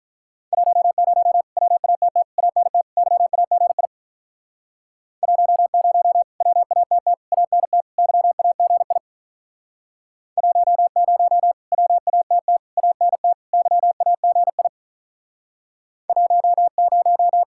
Morse code, heard as a keyed tone, about 44 wpm, 700 Hz. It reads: 10 WATT ANT YAGI 10 WATT ANT YAGI 10 WATT ANT YAGI 10